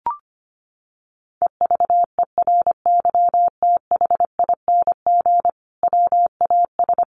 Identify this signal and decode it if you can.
Morse code, heard as a keyed tone, about 25 wpm, 700 Hz. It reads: EVERYTHING WAS
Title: Sentences from Top 1000 Words 25wpm